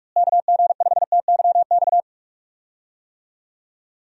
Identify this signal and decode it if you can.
Morse code, heard as a keyed tone, about 46 wpm, 700 Hz. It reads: KG5TYX
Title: USA Call Signs - 6 Characters - Repeated 3x - 45wpm